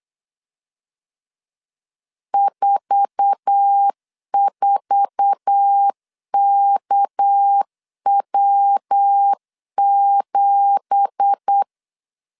Morse code, heard as a keyed tone, about 8 wpm, 800 Hz. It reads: 44KW7
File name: CW / DET / 1 9 1500C S C 8wpm 5653